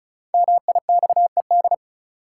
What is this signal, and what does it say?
Morse code, heard as a keyed tone, about 36 wpm, 700 Hz. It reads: MIXED
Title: Sentences from Top 1500 Words - New Word Drill - 35wpm